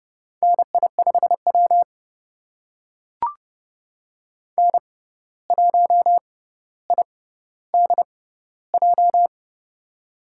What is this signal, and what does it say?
Morse code, heard as a keyed tone, about 30 wpm, spaced as for 14 wpm, 700 Hz. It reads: NI5W N1IDJ
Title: USA Call Signs - 6x Character Spacing - 30wpm